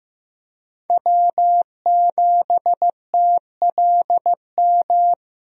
Morse code, heard as a keyed tone, about 15 wpm, 700 Hz. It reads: W7TLM
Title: USA Call Signs - 5 Characters - 15wpm